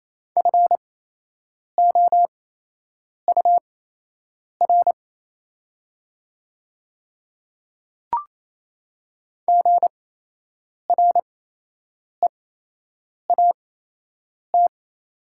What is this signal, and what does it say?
Morse code, heard as a keyed tone, about 28 wpm, spaced as for 8 wpm, 700 Hz. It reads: FOUR GREAT
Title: 8x Character Spacing - Top 300 Words - 28wpm